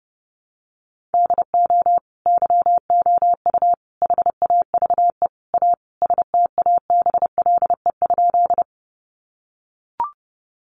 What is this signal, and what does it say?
Morse code, heard as a keyed tone, about 30 wpm, 700 Hz. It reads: DO YOU HAVE A STABLE?
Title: Sentences from Top 1500 Words 30wpm